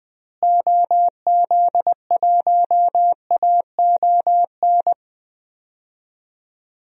Morse code, heard as a keyed tone, about 20 wpm, 700 Hz. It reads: OZ1AON
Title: DXCC-100 Call Signs - 6 Characters - Repeated 3x - 20wpm